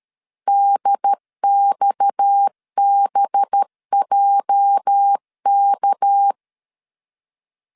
Morse code, heard as a keyed tone, about 13 wpm, 800 Hz. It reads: DXBJK
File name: LEZIONI / 8 YJ45 / 8 YJ 12wpm 2235